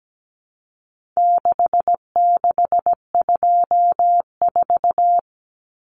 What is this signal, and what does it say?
Morse code, heard as a keyed tone, about 17 wpm, 700 Hz. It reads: 6624
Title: Numbers - 4-Digit 17wpm